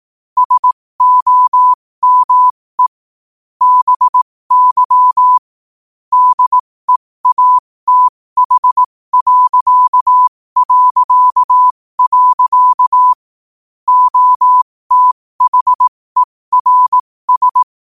Morse code, heard as a keyed tone, about 18 wpm, spaced as for 15 wpm, 1000 Hz. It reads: SOME BY DEATH... OTHERS
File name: qotdfiles / 15wpm